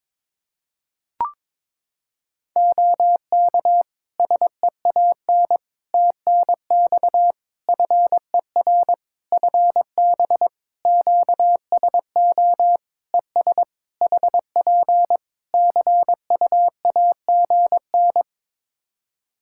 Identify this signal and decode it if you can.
Morse code, heard as a keyed tone, about 22 wpm, 700 Hz. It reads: OK SEAN TNX FER FB QSO ES HP CUAGN
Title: Practice QSO - Exchange 4 - Ending - 22wpm